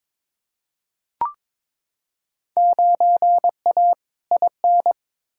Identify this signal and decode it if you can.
Morse code, heard as a keyed tone, about 22 wpm, 700 Hz. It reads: 9A IN
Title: Field Day Exchanges Encore - 22wpm